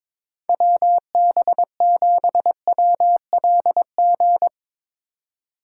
Morse code, heard as a keyed tone, about 22 wpm, 700 Hz. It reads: WB7WLG
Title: Speed Racing - USA Call Signs 22wpm